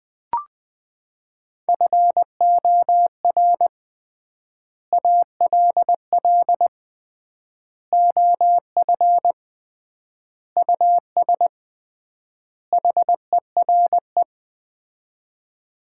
Morse code, heard as a keyed tone, about 20 wpm, 700 Hz. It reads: FOR ALL OF US HERE
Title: 3x Word-Spacing - Sets of 5 Words - 20wpm